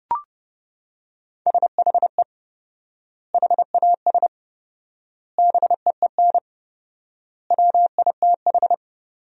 Morse code, heard as a keyed tone, about 30 wpm, 700 Hz. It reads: SHE HAS BEEN WITH
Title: 4x Word-Spacing - Sentences from Top 200 Words - 30wpm